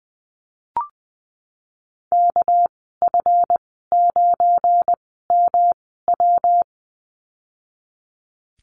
Morse code, heard as a keyed tone, about 20 wpm, spaced as for 15 wpm, 700 Hz. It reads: KF9MW
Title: 213 Lesson 34g - Number 8 - Call Signs 20wpm